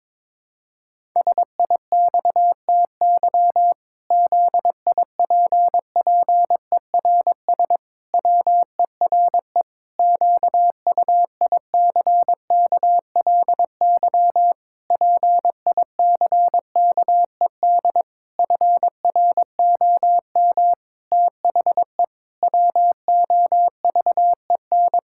Morse code, heard as a keyed tone, about 22 wpm, 700 Hz. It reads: SIXTY ZIPPERS WERE QUICKLY PICKED FROM THE WOVEN
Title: Pangram Challenge v2 - 22wpm